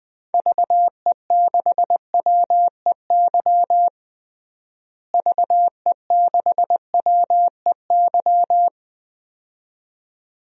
Morse code, heard as a keyed tone, about 20 wpm, 700 Hz. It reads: VE6WEY VE6WEY
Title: DXCC-100 Call Signs - 6 Characters - Repeated 2x - 20wpm